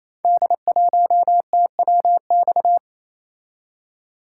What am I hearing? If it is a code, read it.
Morse code, heard as a keyed tone, about 29 wpm, 700 Hz. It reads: D1TWX
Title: Code Groups - 5 Letter:Numbers - 28wpm